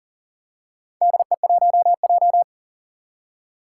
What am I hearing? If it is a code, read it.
Morse code, heard as a keyed tone, about 40 wpm, 700 Hz. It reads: DE1J
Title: Four Challenging Letter-Number Combinations 40wpm